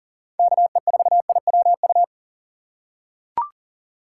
Morse code, heard as a keyed tone, about 40 wpm, 700 Hz. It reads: KE4IWU
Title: USA Call Signs - 6 Characters - 2x Character Spacing - 40wpm